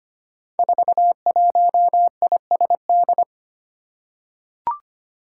Morse code, heard as a keyed tone, about 25 wpm, 700 Hz. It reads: 41ISD
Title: Five Challenging Letter-Number Combinations 25wpm